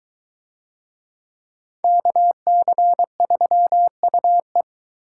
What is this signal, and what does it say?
Morse code, heard as a keyed tone, about 21 wpm, 700 Hz. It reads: KC3UE